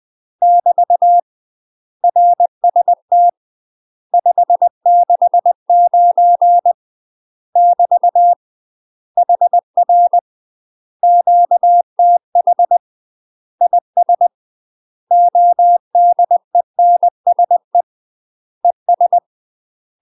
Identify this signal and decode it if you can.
Morse code, heard as a keyed tone, about 20 wpm, 700 Hz. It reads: <BT> RST 569 <BT> HR QTH IS ODENSE ES
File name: UK MASTER QSOs TRAINING / 20WPM / UK MASTER QSOs 0010